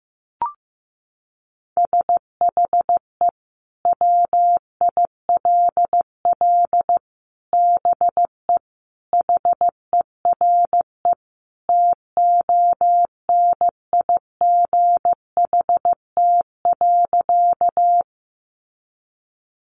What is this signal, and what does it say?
Morse code, heard as a keyed tone, about 15 wpm, 700 Hz. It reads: SHE WILL BE HERE TONIGHT.
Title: Sentences from Top 600 Words 15wpm